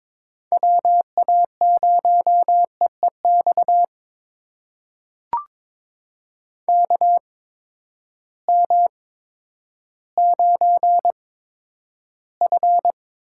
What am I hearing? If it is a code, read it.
Morse code, heard as a keyed tone, about 21 wpm, spaced as for 11 wpm, 700 Hz. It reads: WA0EEX KM9F
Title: USA Call Signs - 8x Character Spacing - 22wpm